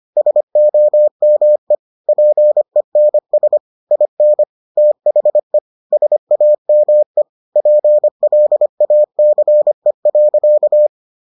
Morse code, heard as a keyed tone, about 25 wpm, 600 Hz. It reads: SOME PENS IN THE SAME PLACE.